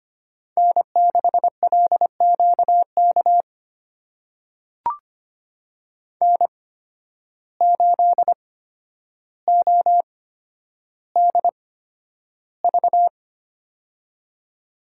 Morse code, heard as a keyed tone, about 25 wpm, spaced as for 11 wpm, 700 Hz. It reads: N6LQK N8ODV
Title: USA Call Signs - 8x Character Spacing - 25wpm